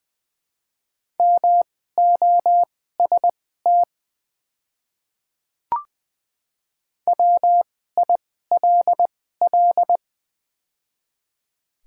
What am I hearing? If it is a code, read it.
Morse code, heard as a keyed tone, about 20 wpm, spaced as for 14 wpm, 690 Hz. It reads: MOST WILL
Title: 2x Character Spacing - Top 100 Words - 20wpm